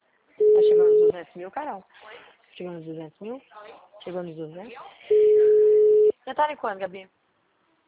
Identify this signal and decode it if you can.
Ringback tone (Europe, 425 Hz), with speech.